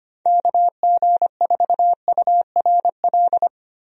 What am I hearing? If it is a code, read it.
Morse code, heard as a keyed tone, about 25 wpm, 700 Hz. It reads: KG4URL
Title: USA Call Signs - 6 Characters - Repeated 2x - 25wpm